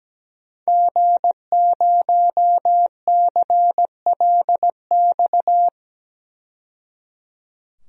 Morse code, heard as a keyed tone, about 17 wpm, 700 Hz. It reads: G0CLX
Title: DXCC-100 Call Signs - 5 Characters - 17wpm